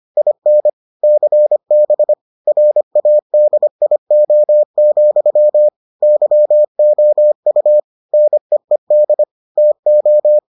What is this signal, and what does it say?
Morse code, heard as a keyed tone, about 25 wpm, 600 Hz. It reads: IN CB RADIO, YOU NEED TO